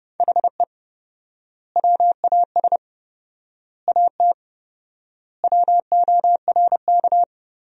Morse code, heard as a keyed tone, about 31 wpm, 700 Hz. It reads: HE WAS AT WORK
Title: 4x Word-Spacing - Sentences from Top 200 Words - 30wpm